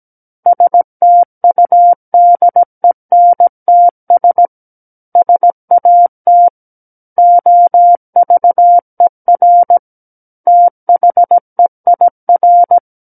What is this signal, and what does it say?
Morse code, heard as a keyed tone, about 17 wpm, 700 Hz. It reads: STUDENTS SAT OVER THEIR
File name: Book One / 08-Chapter 8 - 17 WPM